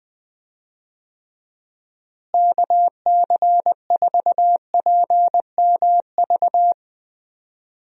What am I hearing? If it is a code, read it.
Morse code, heard as a keyed tone, about 20 wpm, 700 Hz. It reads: KC4PMV
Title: USA Call Signs - 6 Characters - Repeated 3x - 20wpm